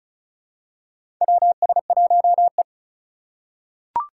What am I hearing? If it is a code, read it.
Morse code, heard as a keyed tone, about 36 wpm, 700 Hz. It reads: WS1E
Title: USA Call Signs - 4 Characters - 3x Character Spacing - 35wpm